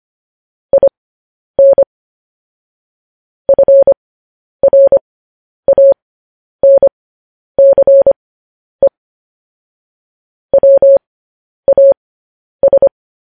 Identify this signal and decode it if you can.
Morse code, heard as a keyed tone, about 25 wpm, spaced as for 10 wpm, 550 Hz. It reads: IN FRANCE WAS
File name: podcast / News headlines-1x-25x10